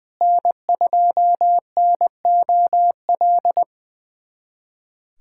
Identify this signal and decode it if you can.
Morse code, heard as a keyed tone, about 20 wpm, 700 Hz. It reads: N2NOL
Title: USA Callsign DE Callsign 20wpm